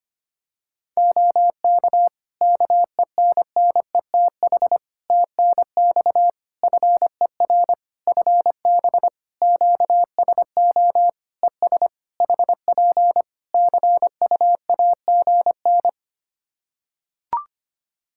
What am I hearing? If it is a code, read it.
Morse code, heard as a keyed tone, about 25 wpm, 700 Hz. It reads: OK KENNETH TNX FER FB QSO ES HP CUAGN
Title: Practice QSO - Exchange 4 - Ending - 25wpm